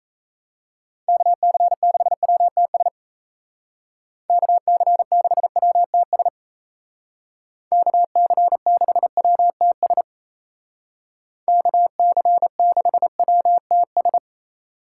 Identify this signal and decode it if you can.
Morse code, heard as a keyed tone, about 33 wpm, 700 Hz. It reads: KC6WTS KC6WTS KC6WTS KC6WTS
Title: Speed Racing - USA 2x3 Call Signs 28wpm